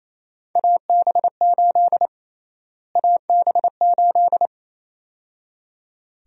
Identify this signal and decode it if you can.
Morse code, heard as a keyed tone, about 28 wpm, 700 Hz. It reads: AB8 AB8
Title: USA Call Sign Prefixes - Repeated 2x - 28wpm